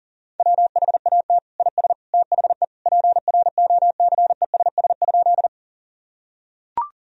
Morse code, heard as a keyed tone, about 40 wpm, 700 Hz. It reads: WHAT IS THE PROCESS?
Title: Sentences from Top 1000 Words - Review Part 3 - 40wpm